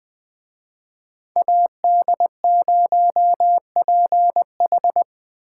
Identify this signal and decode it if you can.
Morse code, heard as a keyed tone, about 20 wpm, 700 Hz. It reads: AD0PH
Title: Speed Racing - USA 2x2 Call Signs 17wpm